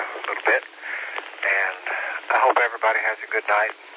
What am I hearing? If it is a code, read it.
Emergency Alert System (SAME) tones, heard as harsh buzzing bursts, with speech. It is heard from 0.7 to 3.4 s.